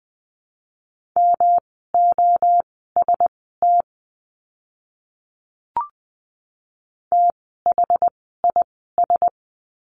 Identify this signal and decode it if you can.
Morse code, heard as a keyed tone, about 20 wpm, spaced as for 14 wpm, 700 Hz. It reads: MOST THIS
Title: 139 Lesson 23f - Letter f - Words 20wpm